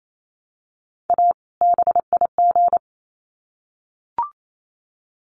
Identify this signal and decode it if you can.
Morse code, heard as a keyed tone, about 28 wpm, 700 Hz. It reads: A BIG
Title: Sets of 2 Words 28wpm